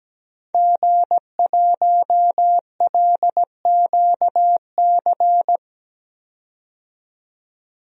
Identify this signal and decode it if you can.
Morse code, heard as a keyed tone, about 17 wpm, 700 Hz. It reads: G1LQC
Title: DXCC-100 Call Signs - 5 Characters - 17wpm